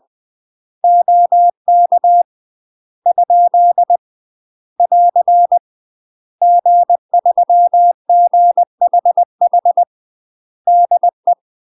Morse code, heard as a keyed tone, about 20 wpm, 700 Hz. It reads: OK ? <AR> G3GHH DE
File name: UK MASTER QSOs TRAINING / 20WPM / UK MASTER QSOs 0049